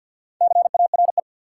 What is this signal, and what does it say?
Morse code, heard as a keyed tone, about 53 wpm, 700 Hz. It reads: CARE
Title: Sets of 3 Words - New Word Drill - 50wpm